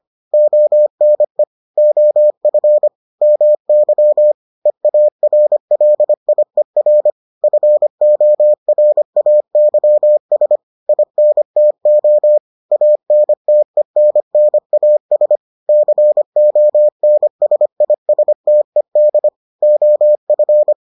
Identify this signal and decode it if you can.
Morse code, heard as a keyed tone, about 25 wpm, 600 Hz. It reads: ONE OF MY EARLIER FORAYS INTO ANTENNAS CONSISTED OF